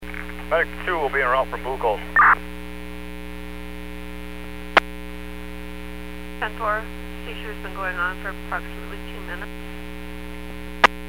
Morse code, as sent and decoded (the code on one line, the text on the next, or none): none